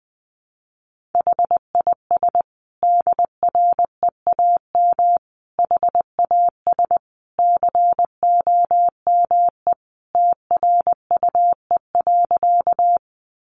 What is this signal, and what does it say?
Morse code, heard as a keyed tone, about 20 wpm, 700 Hz. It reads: HIS DREAM HAS COME TRUE.
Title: Sentences from Top 1500 Words 20wpm